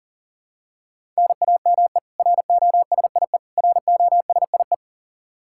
Morse code, heard as a keyed tone, about 40 wpm, 700 Hz. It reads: NAME ROSIE ROSIE
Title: K1USN SST QSO Practice 40wpm